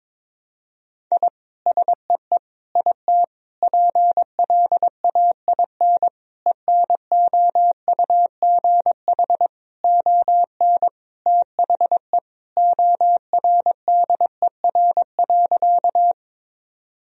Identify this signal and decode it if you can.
Morse code, heard as a keyed tone, about 22 wpm, 700 Hz. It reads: I SEE IT PLAIN ENOUGH ON THE ORDER.
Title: Sentences from Top 1000 Words - Review Part 2 - 22wpm